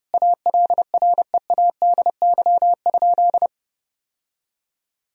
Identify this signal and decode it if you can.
Morse code, heard as a keyed tone, about 30 wpm, 700 Hz. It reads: ALREADY?
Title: Sentences from Top 1000 Words - Review Part 3 - 30wpm